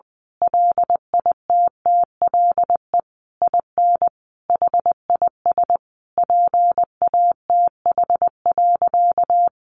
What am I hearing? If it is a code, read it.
Morse code, heard as a keyed tone, about 20 wpm, 700 Hz. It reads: LITTLE IN HIS PATH.
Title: Sentences from Top 1400 Words 20wpm